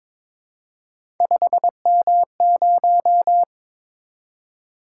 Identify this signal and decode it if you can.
Morse code, heard as a keyed tone, about 22 wpm, 700 Hz. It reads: 5M0
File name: practice / 22wpm / Three Letter-Number Combinations - 22wpm